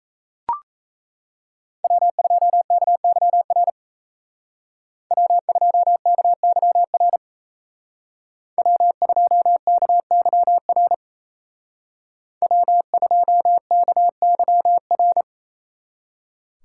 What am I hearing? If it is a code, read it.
Morse code, heard as a keyed tone, about 36 wpm, 700 Hz. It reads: W2KYR W2KYR W2KYR W2KYR